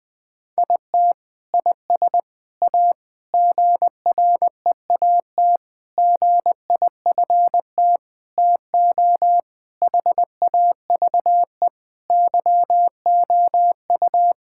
Morse code, heard as a keyed tone, about 21 wpm, 700 Hz. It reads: IT IS A GREAT GIFT TO HAVE YOU
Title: Sentences from Top 1000 Words - Review Part 6 - 20wpm